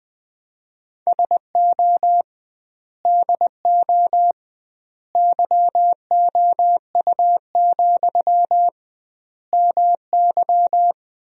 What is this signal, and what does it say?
Morse code, heard as a keyed tone, about 20 wpm, 700 Hz. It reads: SO DO YOU, MY
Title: 2x Word-Spacing - Sentences from Top 400 Words - 20wpm